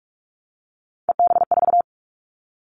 Morse code, heard as a keyed tone, about 44 wpm, 700 Hz. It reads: EB4
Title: Callsign Prefixes for Top 25 DXCC - Repeated 3x - 45wpm